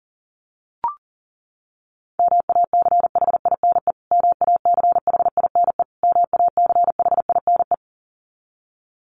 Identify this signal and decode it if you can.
Morse code, heard as a keyed tone, about 40 wpm, 700 Hz. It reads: MACHINE MACHINE MACHINE
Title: Top 500 Words Repeated 40wpm